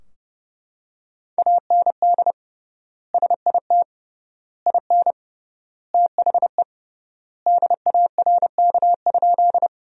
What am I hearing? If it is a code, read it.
Morse code, heard as a keyed tone, about 30 wpm, 700 Hz. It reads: AND SIT IN THE DARK?
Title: 3x Word-Spacing - Sentences from Top 500 Words - 30wpm